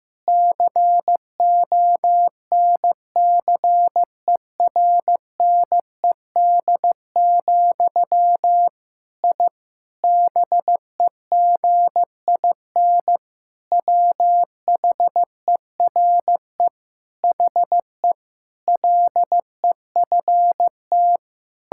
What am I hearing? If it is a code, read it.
Morse code, heard as a keyed tone, about 15 wpm, 700 Hz. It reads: CONCERNED, I BEGIN WHERE HE LEFT
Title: Sentences from Top 1000 Words - Review Part 6 - 15wpm